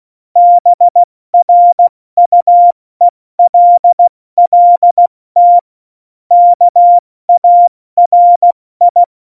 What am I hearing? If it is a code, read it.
Morse code, heard as a keyed tone, about 16 wpm, 700 Hz. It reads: BRUELLT KARI
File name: Morsekurs / Witze / Witz 31. Oktober 17, Tempo 70